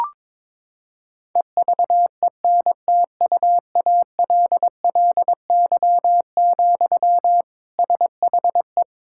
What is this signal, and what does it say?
Morse code, heard as a keyed tone, about 22 wpm, 700 Hz. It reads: EVENTUALLY, SHE
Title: Sentences from Top 1000 Words - Review Part 6 - 22wpm